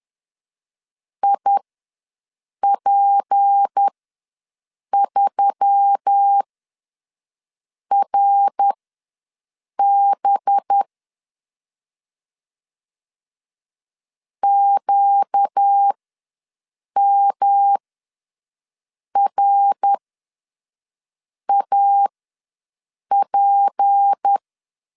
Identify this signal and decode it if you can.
Morse code, heard as a keyed tone, about 10 wpm, spaced as for 5 wpm, 800 Hz. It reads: IP3RB QMRAP
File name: LEZIONI / 6 PKX0 / 6 P ws 10wpm 3751